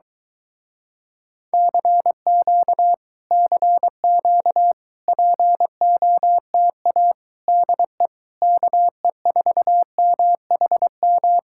Morse code, heard as a keyed tone, about 23 wpm, 700 Hz. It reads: CQ CQ POTA DE KE4MHM